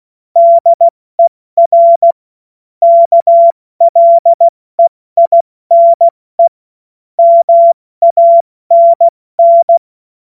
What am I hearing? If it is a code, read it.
Morse code, heard as a keyed tone, about 16 wpm, spaced as for 14 wpm, 680 Hz. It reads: DER KLEINE MANN
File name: Morsekurs / Witze / Witz 22. Februar 2022 70 F80